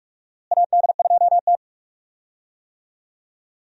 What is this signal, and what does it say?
Morse code, heard as a keyed tone, about 42 wpm, 700 Hz. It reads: AD2T
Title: USA Call Signs - 4 Characters - Repeated 3x - 45wpm